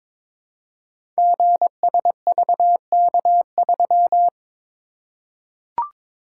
Morse code, heard as a keyed tone, about 22 wpm, 700 Hz. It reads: GSVK3
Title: Code Groups - 5 Letter:Numbers - 22wpm